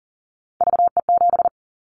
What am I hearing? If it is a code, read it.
Morse code, heard as a keyed tone, about 40 wpm, 700 Hz. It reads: VE7
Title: Callsign Prefixes for Top 100 DXCC - 40wpm